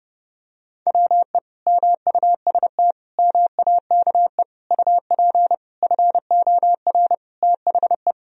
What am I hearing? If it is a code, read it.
Morse code, heard as a keyed tone, about 30 wpm, 700 Hz. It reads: WE MUST MAKE UP FOR THE
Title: Sentences from Top 1000 Words - Review Part 3 - 30wpm